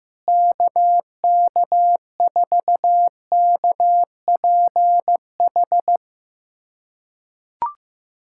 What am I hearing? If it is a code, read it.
Morse code, heard as a keyed tone, about 15 wpm, 700 Hz. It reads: KK4KPH
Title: USA Callsign DE Callsign 15wpm